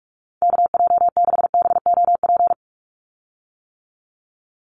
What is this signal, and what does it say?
Morse code, heard as a keyed tone, about 44 wpm, 700 Hz. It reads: KJ6BOP
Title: USA Call Signs 45wpm